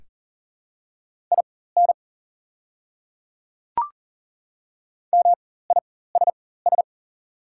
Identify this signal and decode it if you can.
Morse code, heard as a keyed tone, about 38 wpm, spaced as for 18 wpm, 690 Hz. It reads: IN MISS
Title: 4x Character Spacing - Top 500 Words - 40wpm